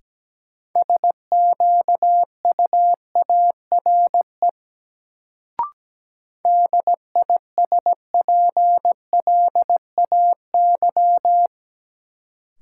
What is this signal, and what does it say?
Morse code, heard as a keyed tone, about 17 wpm, 700 Hz. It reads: SQUARE DISPLAY
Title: Sentences from Top 1200 Words - New Word Drill Alt - 17wpm